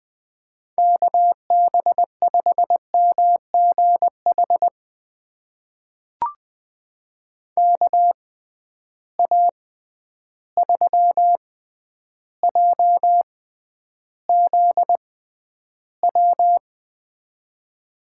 Morse code, heard as a keyed tone, about 20 wpm, spaced as for 10 wpm, 700 Hz. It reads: KB5MGH KA3JZW
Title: USA Call Signs - 6 Characters - 6x Character Spacing - 20wpm